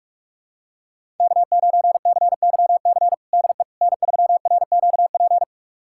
Morse code, heard as a keyed tone, about 45 wpm, 700 Hz. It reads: K9CYC DE N3RQP
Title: USA Callsign DE Callsign - 5 Characters - 45wpm